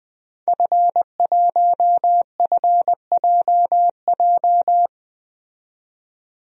Morse code, heard as a keyed tone, about 21 wpm, 700 Hz. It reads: F1FJJ
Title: DXCC-100 Call Signs - 5 Characters - 20wpm